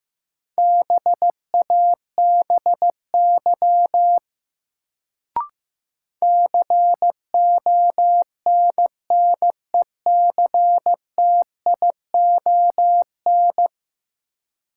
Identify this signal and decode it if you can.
Morse code, heard as a keyed tone, about 15 wpm, 700 Hz. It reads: BABY CONNECTION
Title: Sentences from Top 1300 Words - New Word Drill Alt - 15wpm